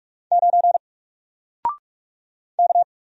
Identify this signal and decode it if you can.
Morse code, heard as a keyed tone, about 44 wpm, 700 Hz. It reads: 9 K
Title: Single Character (Letter:Number:Punctuation:Prosign) Alt 45wpm